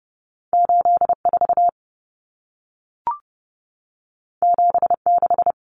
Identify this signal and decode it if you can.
Morse code, heard as a keyed tone, about 30 wpm, 700 Hz. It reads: 84 76
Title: Numbers - 2-Digit 30wpm